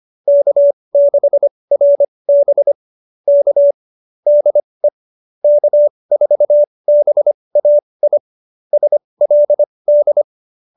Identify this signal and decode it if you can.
Morse code, heard as a keyed tone, about 25 wpm, spaced as for 20 wpm, 570 Hz. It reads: K6RB K DE K4BAI SLD
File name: QSO301 20